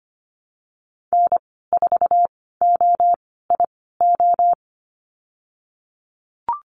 Morse code, heard as a keyed tone, about 25 wpm, spaced as for 16 wpm, 700 Hz. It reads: N4OIO
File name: Lesson 15g - Number 5 - Call Signs 25wpm